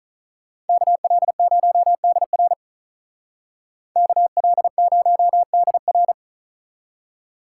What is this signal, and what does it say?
Morse code, heard as a keyed tone, about 46 wpm, 700 Hz. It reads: KL0DR KL0DR